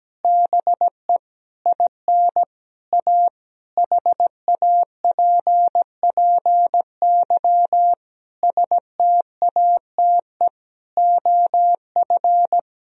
Morse code, heard as a keyed tone, about 17 wpm, 700 Hz. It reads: BE IN A HAPPY STATE OF
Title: Sentences from Top 1000 Words - Review Part 2 - 17wpm